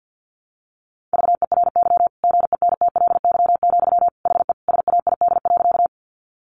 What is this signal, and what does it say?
Morse code, heard as a keyed tone, about 50 wpm, 700 Hz. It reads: VERY GENTLY, HE SAID.
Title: Sentences from Top 1500 Words 50wpm